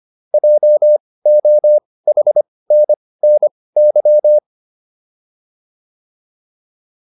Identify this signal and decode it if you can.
Morse code, heard as a keyed tone, about 25 wpm, spaced as for 18 wpm, 600 Hz. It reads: JOHNNY
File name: QSO306 18